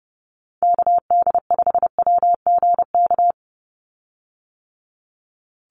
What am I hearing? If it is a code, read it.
Morse code, heard as a keyed tone, about 30 wpm, 700 Hz. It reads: KD5WGK
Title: USA Call Signs 30wpm